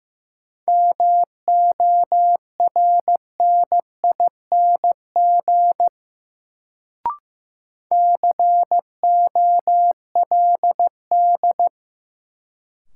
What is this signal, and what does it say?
Morse code, heard as a keyed tone, about 15 wpm, 700 Hz. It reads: MORNING COLD
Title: Sentences from Top 400 Words - New Word Drill - 15wpm